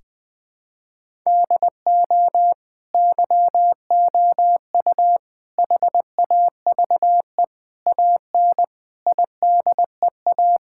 Morse code, heard as a keyed tone, about 21 wpm, 700 Hz. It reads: DO YOU HAVE AN IDEA
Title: Sets of 9 Words - 20wpm